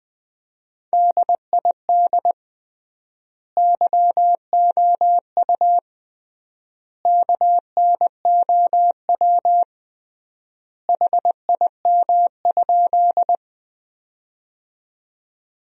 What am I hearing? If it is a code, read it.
Morse code, heard as a keyed tone, about 20 wpm, 700 Hz. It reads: DID YOU KNOW HIM?
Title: 3x Word-Spacing - Sentences from Top 100 Words - 20wpm